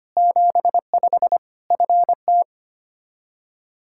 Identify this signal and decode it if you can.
Morse code, heard as a keyed tone, about 25 wpm, 700 Hz. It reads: 75 FT
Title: Practice QSO - Exchange 2 - Info - 25wpm